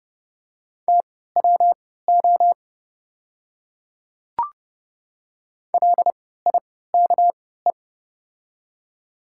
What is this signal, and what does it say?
Morse code, heard as a keyed tone, about 30 wpm, spaced as for 17 wpm, 700 Hz. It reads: TWO LIKE
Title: 3x Character Spacing - Top 100 Words - 30wpm